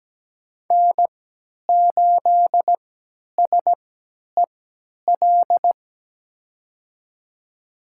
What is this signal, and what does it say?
Morse code, heard as a keyed tone, about 17 wpm, spaced as for 10 wpm, 700 Hz. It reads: N8SEL
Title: USA Call Signs - 5 Characters - 3x Character Spacing - 17wpm